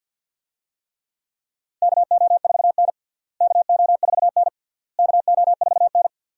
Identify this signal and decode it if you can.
Morse code, heard as a keyed tone, about 53 wpm, 700 Hz. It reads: KO4N KO4N KO4N
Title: USA Call Signs - 4 Characters - Repeated 3x - 50wpm